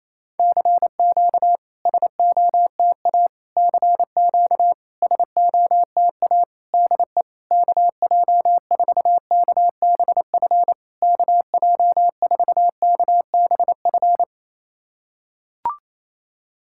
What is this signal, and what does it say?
Morse code, heard as a keyed tone, about 28 wpm, 700 Hz. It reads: CQ SOTA CQ SOTA DE KJ4KBF KJ4KBF
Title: SOTA QSO Practice 28wpm